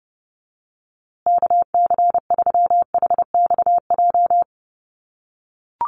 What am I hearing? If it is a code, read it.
Morse code, heard as a keyed tone, about 30 wpm, 700 Hz. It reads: KC3HXJ
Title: USA Call Signs Repeated 3x 30wpm